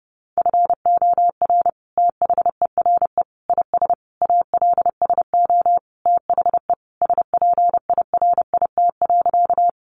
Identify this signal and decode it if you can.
Morse code, heard as a keyed tone, about 30 wpm, 700 Hz. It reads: FOR THERE IS ALSO THE SPIRIT.
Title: Sentences from Top 1400 Words 30wpm